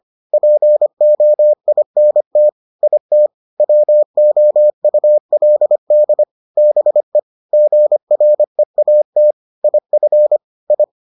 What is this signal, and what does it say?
Morse code, heard as a keyed tone, about 25 wpm, 600 Hz. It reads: POINT IT WOULD BE GREAT IF I